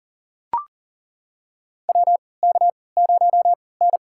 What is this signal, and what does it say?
Morse code, heard as a keyed tone, about 40 wpm, spaced as for 23 wpm, 700 Hz. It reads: WK0N
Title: USA Call Signs - 4 Characters - 3x Character Spacing - 40wpm